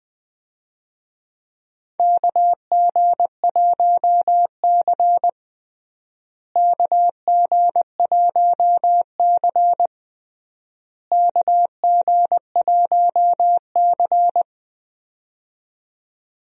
Morse code, heard as a keyed tone, about 20 wpm, 700 Hz. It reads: KG1C KG1C KG1C
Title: USA Call Signs - 4 Characters - Repeated 3x - 20wpm